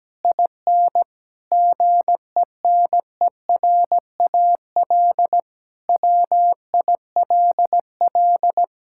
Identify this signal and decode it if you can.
Morse code, heard as a keyed tone, about 17 wpm, 700 Hz. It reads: IN GENERAL WILL
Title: Sentences from Top 1000 Words - Review Part 6 - 17wpm